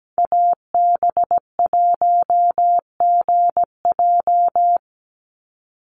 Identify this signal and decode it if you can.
Morse code, heard as a keyed tone, about 17 wpm, 700 Hz. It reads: AB1GJ
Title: Call Signs Repeated 3x - 17wpm